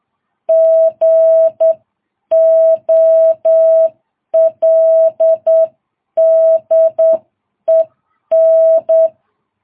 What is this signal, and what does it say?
Morse code, heard as a keyed tone, about 8 wpm, 660 Hz. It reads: GOLDEN
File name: jh8cbh